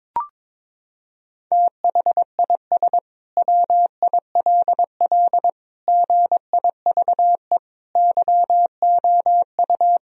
Morse code, heard as a keyed tone, about 22 wpm, 700 Hz. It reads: THIS WILL GIVE YOU
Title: Sentences from Top 1000 Words - Review Part 3 - 22wpm